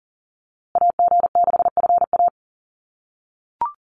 Morse code, heard as a keyed tone, about 40 wpm, 700 Hz. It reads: AG6FA
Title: USA Call Signs Repeated 3x 40wpm